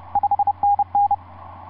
Morse code, heard as a keyed tone, about 30 wpm, 810 Hz. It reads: 5NN
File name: s79gm on4on etc 15m cw 30mar2010 0653